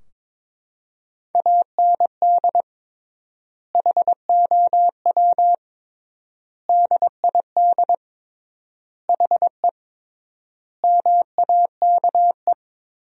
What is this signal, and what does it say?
Morse code, heard as a keyed tone, about 22 wpm, 700 Hz. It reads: AND HOW DID HE MAKE
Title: 3x Word-Spacing - Sentences from Top 400 Words - 22wpm